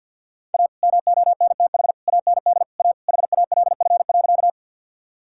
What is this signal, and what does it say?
Morse code, heard as a keyed tone, about 50 wpm, 700 Hz. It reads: A MONTH AND A HALF.
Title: Sentences from Top 1000 Words - Review Part 5 - 50wpm